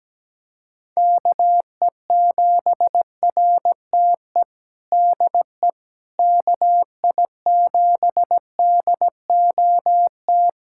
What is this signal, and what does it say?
Morse code, heard as a keyed tone, about 17 wpm, 700 Hz. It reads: KE7RTE DE KI7DOT
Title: USA Callsign DE Callsign - 6 Characters - 17wpm